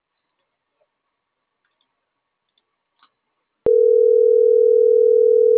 Ringback tone (North America), 440 + 480 Hz.